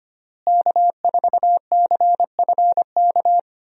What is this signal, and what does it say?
Morse code, heard as a keyed tone, about 25 wpm, 700 Hz. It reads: K4CFK
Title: USA Call Signs - 5 Characters - Repeated 3x - 25wpm